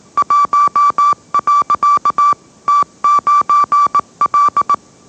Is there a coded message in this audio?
Yes: Morse code from 0.2 to 4.7 s, heard as a keyed tone, about 18 wpm, 1160 Hz. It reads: 1. T9L